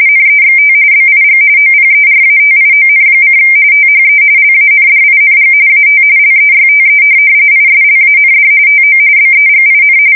RTTY, about 45.45 baud. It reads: NEXT BIG OPERATION: THE LANDING AT LEYTE ISLAND TO BEGIN
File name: fsk170 / USS-New-Jersey